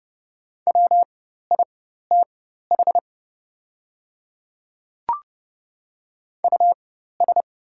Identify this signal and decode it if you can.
Morse code, heard as a keyed tone, about 30 wpm, spaced as for 14 wpm, 700 Hz. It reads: WITH US
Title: 4x Character Spacing - Top 300 Words - 30wpm